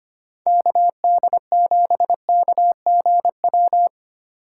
Morse code, heard as a keyed tone, about 25 wpm, 700 Hz. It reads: KD7KGW
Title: USA Call Signs - 4x Character Spacing - 25wpm